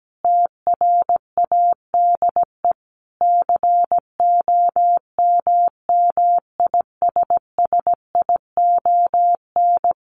Morse code, heard as a keyed tone, about 17 wpm, 700 Hz. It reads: TRADE COMMISSION
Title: Sentences from Top 1400 Words 17wpm